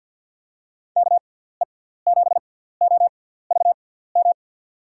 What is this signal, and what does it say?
Morse code, heard as a keyed tone, about 47 wpm, spaced as for 17 wpm, 700 Hz. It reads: KE7OVM
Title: USA Call Signs - 6 Characters - 6x Character Spacing - 50wpm